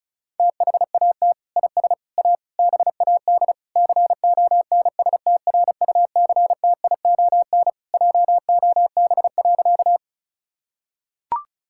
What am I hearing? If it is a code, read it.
Morse code, heard as a keyed tone, about 36 wpm, 700 Hz. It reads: THAT IS A BAD CONSTRUCTION JOB.
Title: Sentences from Top 1600 Words 35wpm